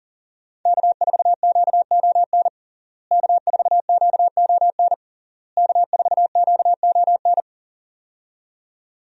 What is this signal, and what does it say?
Morse code, heard as a keyed tone, about 40 wpm, 690 Hz. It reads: K4QON K4QON K4QON
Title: USA Call Signs - 5 Characters - Repeated 3x - 40wpm